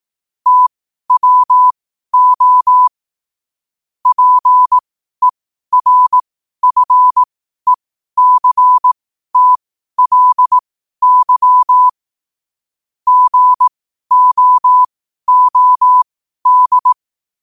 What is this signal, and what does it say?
Morse code, heard as a keyed tone, about 18 wpm, spaced as for 12 wpm, 1000 Hz. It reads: TWO PERFECTLY GOOD
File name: qotdfiles / 12wpm